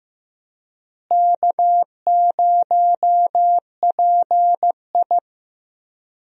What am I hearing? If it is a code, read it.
Morse code, heard as a keyed tone, about 15 wpm, 700 Hz. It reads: K0PI
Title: USA Call Signs - 4 Characters - Repeated 3x - 15wpm